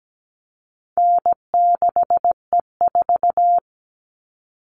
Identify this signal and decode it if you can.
Morse code, heard as a keyed tone, about 17 wpm, 700 Hz. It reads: N6E4
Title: All Cut Numbers - 4-Digit 17wpm